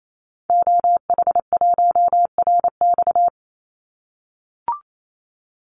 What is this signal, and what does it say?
Morse code, heard as a keyed tone, about 28 wpm, 700 Hz. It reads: OH1RX
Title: Call Signs - 28wpm